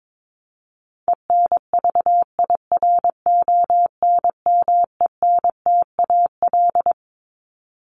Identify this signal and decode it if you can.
Morse code, heard as a keyed tone, about 22 wpm, 700 Hz. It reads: ENVIRONMENTAL
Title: Top 1001-1250 Words 22wpm